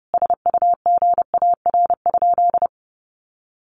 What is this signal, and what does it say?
Morse code, heard as a keyed tone, about 30 wpm, 700 Hz. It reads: SUGAR?
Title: Sentences from Top 1200 Words 30wpm